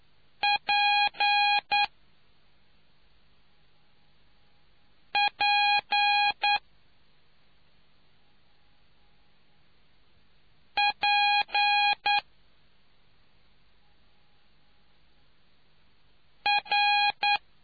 Morse code, heard as a keyed tone, about 9 wpm, spaced as for 2 wpm, 820 Hz. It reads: PP P R